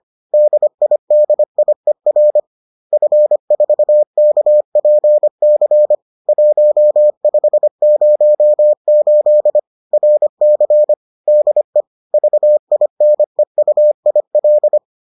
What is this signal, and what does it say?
Morse code, heard as a keyed tone, about 25 wpm, 600 Hz. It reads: DIDIER F4KPC 1508 RC DE VINEUIL
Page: bulletin UFT sous forme texte et en CW